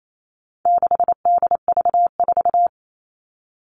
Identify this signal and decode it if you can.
Morse code, heard as a keyed tone, about 28 wpm, 700 Hz. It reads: 6DV4
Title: All Cut Numbers - 4-Digit 28wpm